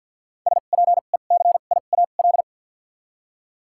Morse code, heard as a keyed tone, about 53 wpm, 700 Hz. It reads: S P E C I A L
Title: 2x Character Spacing - Top 500 Words - 50wpm